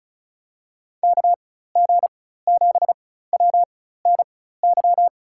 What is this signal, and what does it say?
Morse code, heard as a keyed tone, about 34 wpm, spaced as for 16 wpm, 700 Hz. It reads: KG7WNY
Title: USA Call Signs - 6 Characters - 4x Character Spacing - 35wpm